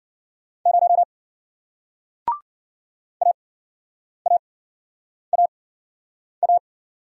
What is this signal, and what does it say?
Morse code, heard as a keyed tone, about 53 wpm, 700 Hz. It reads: 0 A A A A